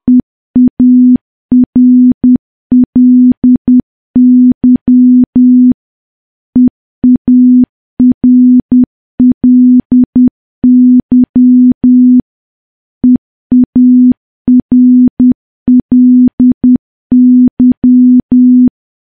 Morse code, heard as a keyed tone, about 10 wpm, 250 Hz. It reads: EARLY EARLY EARLY
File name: morse / morse-training-10WPM